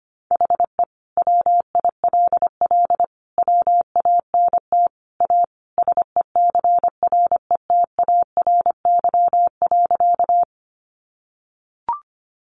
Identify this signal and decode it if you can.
Morse code, heard as a keyed tone, about 25 wpm, 700 Hz. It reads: HE WILL WANT A SECRETARY.
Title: Sentences from Top 1300 Words 25wpm